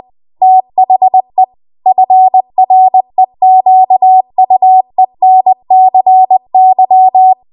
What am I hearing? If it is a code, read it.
Morse code, heard as a keyed tone, about 20 wpm, 750 Hz. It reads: THE FREQUENCY
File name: Archive / 20 WPM / 20WPM